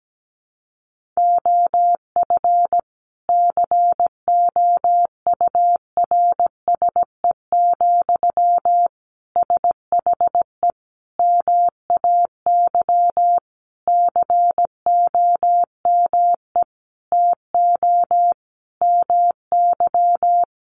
Morse code, heard as a keyed tone, about 17 wpm, 700 Hz. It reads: OF COURSE, SHE MAY COME TO MY
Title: Sentences from Top 1400 Words 17wpm